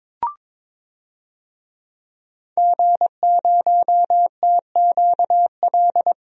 Morse code, heard as a keyed tone, about 22 wpm, 700 Hz. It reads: G0TQL
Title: DXCC-100 Call Signs - 5 Characters - 22wpm